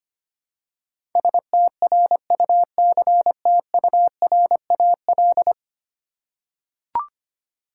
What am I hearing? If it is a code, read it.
Morse code, heard as a keyed tone, about 25 wpm, 700 Hz. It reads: STRUCTURAL
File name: Ten Letter Words - 25wpm